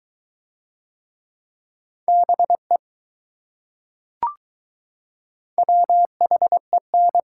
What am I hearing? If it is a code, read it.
Morse code, heard as a keyed tone, about 20 wpm, 700 Hz. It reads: BE WHEN